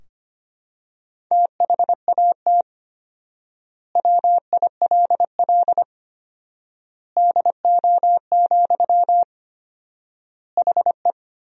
Morse code, heard as a keyed tone, about 25 wpm, 700 Hz. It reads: THAT WILL DO, HE
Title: 4x Word-Spacing - Sentences from Top 100 Words - 25wpm